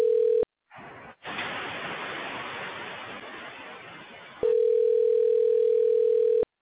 Ringback tone (North America), 440 + 480 Hz.